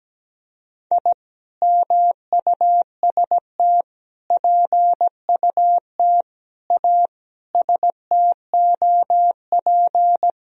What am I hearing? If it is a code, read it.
Morse code, heard as a keyed tone, about 17 wpm, 700 Hz. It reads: I MUST PUT A STOP
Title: Sentences from Top 1000 Words - Review Part 1 - 17wpm